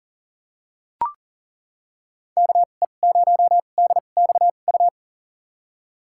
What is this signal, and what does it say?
Morse code, heard as a keyed tone, about 40 wpm, spaced as for 29 wpm, 700 Hz. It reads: KE0DXU
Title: USA Call Signs - 2x Character Spacing - 40wpm